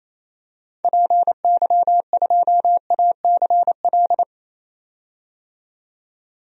Morse code, heard as a keyed tone, about 28 wpm, 700 Hz. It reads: PY2ACL
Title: DXCC-100 Call Signs - 6 Characters - 28wpm